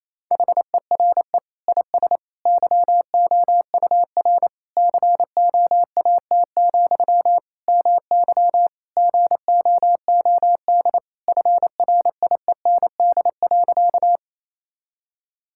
Morse code, heard as a keyed tone, about 28 wpm, 700 Hz. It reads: HERE IS YOUR COAT, MY GOOD FRIEND.
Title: Sentences from Top 1600 Words 28wpm